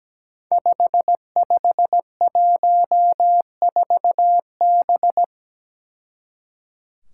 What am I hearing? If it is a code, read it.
Morse code, heard as a keyed tone, about 17 wpm, 700 Hz. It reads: 5514B
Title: Five Challenging Letter-Number Combinations 17wpm